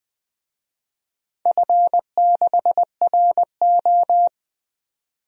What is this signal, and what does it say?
Morse code, heard as a keyed tone, about 20 wpm, 700 Hz. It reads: F6RO
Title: DXCC-100 Call Signs - 4 Characters - 20wpm